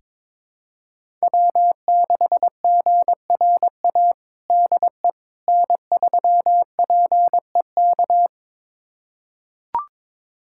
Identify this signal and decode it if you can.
Morse code, heard as a keyed tone, about 21 wpm, 700 Hz. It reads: W6GRA DE N3PEK
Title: USA Callsign DE Callsign 22wpm